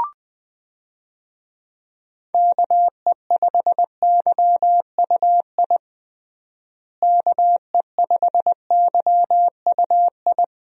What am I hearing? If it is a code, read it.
Morse code, heard as a keyed tone, about 20 wpm, 700 Hz. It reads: KE5YUI KE5YUI
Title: USA Call Signs - 6 Characters - Repeated 2x - 20wpm